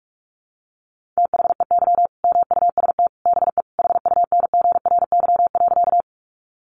Morse code, heard as a keyed tone, about 44 wpm, 700 Hz. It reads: THEY MUST BE HUNGRY.
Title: Sentences from Top 1400 Words 45wpm